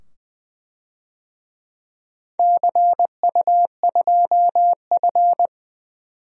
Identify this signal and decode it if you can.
Morse code, heard as a keyed tone, about 20 wpm, 700 Hz. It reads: CU2F
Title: DXCC-100 Call Signs - 4 Characters - 20wpm